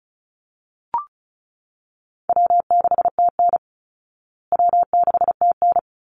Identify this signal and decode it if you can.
Morse code, heard as a keyed tone, about 34 wpm, 700 Hz. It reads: W6TN W6TN
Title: Easy Call Signs Repeated 2x - 35wpm